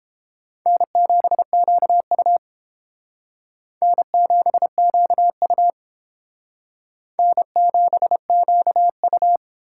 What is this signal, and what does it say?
Morse code, heard as a keyed tone, about 29 wpm, 700 Hz. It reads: N7QU N7QU N7QU